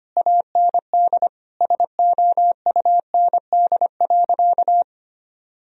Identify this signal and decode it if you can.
Morse code, heard as a keyed tone, about 25 wpm, 700 Hz. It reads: AND SOUND.
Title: Sentences from Top 1000 Words - Review Part 2 - 25wpm